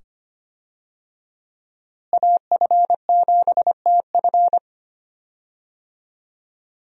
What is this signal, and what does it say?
Morse code, heard as a keyed tone, about 25 wpm, 700 Hz. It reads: AF7TF
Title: USA Call Signs - 5 Characters - Repeated 2x - 25wpm